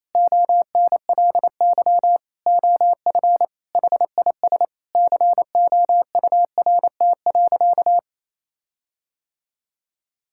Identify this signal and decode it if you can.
Morse code, heard as a keyed tone, about 28 wpm, 700 Hz. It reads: ONLY OF HIS COURT.
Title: Sentences from Top 1600 Words 28wpm